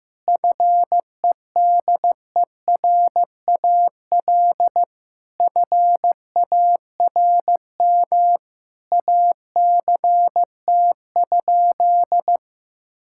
Morse code, heard as a keyed tone, about 15 wpm, 700 Hz. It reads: FEDERAL FARM ACT?
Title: Sentences from Top 1000 Words - Review Part 5 - 15wpm